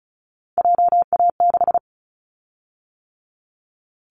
Morse code, heard as a keyed tone, about 34 wpm, 700 Hz. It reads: JA6
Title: Callsign Prefixes for Top 25 DXCC - 35wpm